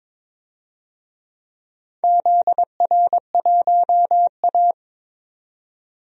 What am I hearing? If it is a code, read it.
Morse code, heard as a keyed tone, about 22 wpm, 700 Hz. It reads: ZR1A
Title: DXCC-100 Call Signs - 4 Characters - 22wpm